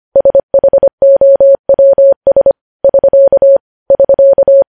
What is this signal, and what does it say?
Morse code, heard as a keyed tone, about 25 wpm, 550 Hz. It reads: SHOWS <SK> <SK>
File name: podcast / News headlines-1x-25x25